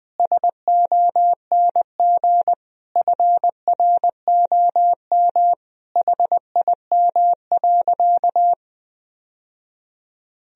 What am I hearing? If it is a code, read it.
Morse code, heard as a keyed tone, about 20 wpm, 700 Hz. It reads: SONG FROM HIM.
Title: Sentences from Top 1000 Words - Review Part 2 - 20wpm